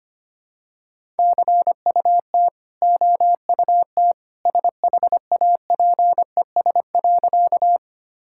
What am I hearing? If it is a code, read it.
Morse code, heard as a keyed tone, about 25 wpm, 700 Hz. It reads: CUT OUT SHAPES.
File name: Easy Sentences with Hints - 25wpm